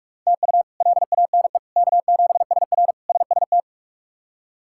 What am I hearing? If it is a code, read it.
Morse code, heard as a keyed tone, about 46 wpm, 700 Hz. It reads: TU LANE K7SR SST
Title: K1USN SST QSO Practice - 45wpm